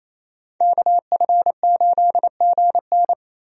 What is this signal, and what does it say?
Morse code, heard as a keyed tone, about 28 wpm, 700 Hz. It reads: KF8GN
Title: USA Call Signs - 5 Characters - Repeated 2x - 28wpm